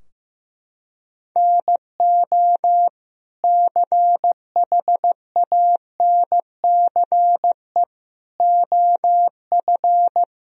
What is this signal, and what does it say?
Morse code, heard as a keyed tone, about 15 wpm, 700 Hz. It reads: NO CHANCE OF
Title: Sentences from Top 1000 Words - Review Part 2 - 15wpm